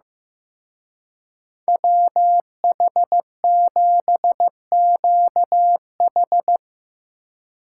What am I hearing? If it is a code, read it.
Morse code, heard as a keyed tone, about 17 wpm, 700 Hz. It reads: WH7QH
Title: Speed Racing - USA 2x2 Call Signs 15wpm